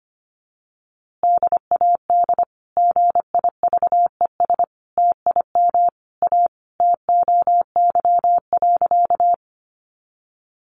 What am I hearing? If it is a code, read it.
Morse code, heard as a keyed tone, about 25 wpm, 700 Hz. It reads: DAD GIVES TIM A TOY.
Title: Easy Phrases 25wpm